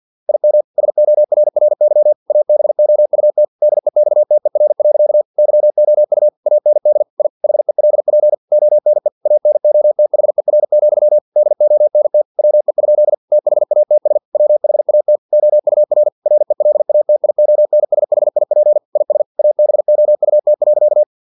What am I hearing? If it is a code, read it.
Morse code, heard as a keyed tone, about 50 wpm, 600 Hz. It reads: IM SORRY ABOUT DEXTER. YOU AND I HELP ONE ANOTHER, DONT WE? THATS WHAT OUR RELATIONSHIP IS ABOUT.
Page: Never 49wpm